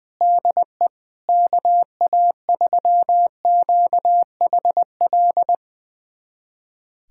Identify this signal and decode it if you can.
Morse code, heard as a keyed tone, about 20 wpm, 700 Hz. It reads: DE KA3QHL
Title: USA Callsign DE Callsign 20wpm